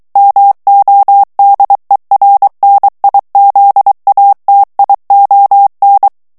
Morse code, heard as a keyed tone, about 23 wpm, 800 Hz. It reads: MODERNIZATION
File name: ANGLAIS texte clair 3 vitesse 22 mots minute